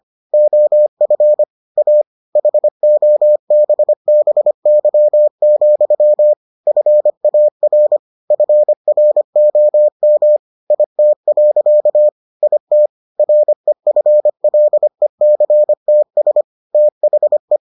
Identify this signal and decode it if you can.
Morse code, heard as a keyed tone, about 25 wpm, 600 Hz. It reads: OF A HOBBY, FAR FROM IT. IT REFLECTS THE